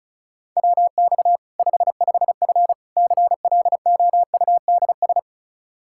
Morse code, heard as a keyed tone, about 34 wpm, 700 Hz. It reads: WX 55F CLOUDS
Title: Practice QSO - Exchange 2 - Info - 35wpm